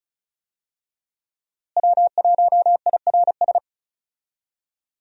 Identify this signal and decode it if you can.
Morse code, heard as a keyed tone, about 34 wpm, 700 Hz. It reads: W1IRS
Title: USA Call Signs - 5 Characters - Repeated 2x - 35wpm